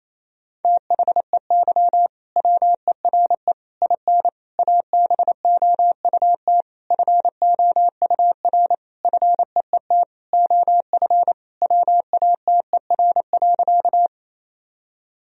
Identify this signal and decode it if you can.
Morse code, heard as a keyed tone, about 27 wpm, 700 Hz. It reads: THEY WERE IN ABOUT FOUR FEET OF WATER.
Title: Sentences from Top 1000 Words - Review Part 1 - 28wpm